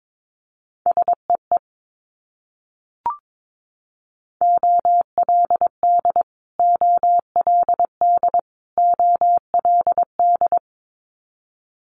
Morse code, heard as a keyed tone, about 22 wpm, 700 Hz. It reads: SEE OLD OLD OLD
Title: Multi-letter Phonemes Repeated 3x 22wpm